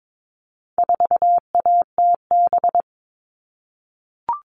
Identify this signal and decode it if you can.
Morse code, heard as a keyed tone, about 22 wpm, 700 Hz. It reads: 4ATB
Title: All Cut Numbers - 4-Digit 22wpm